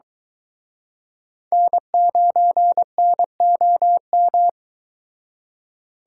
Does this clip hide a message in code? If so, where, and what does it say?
Yes: Morse code from 1.5 to 4.5 s, heard as a keyed tone, about 23 wpm, 700 Hz. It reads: N9NOM